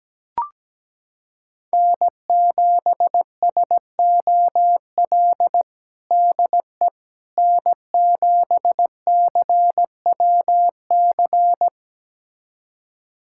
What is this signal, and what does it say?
Morse code, heard as a keyed tone, about 17 wpm, 700 Hz. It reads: N7SOL DE N7CWC
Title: USA Callsign DE Callsign - 5 Characters - 17wpm